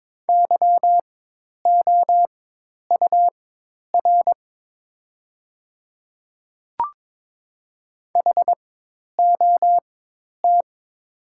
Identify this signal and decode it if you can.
Morse code, heard as a keyed tone, about 22 wpm, spaced as for 10 wpm, 700 Hz. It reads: YOUR HOT
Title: 4x Character Spacing - Top 100 Words - 22wpm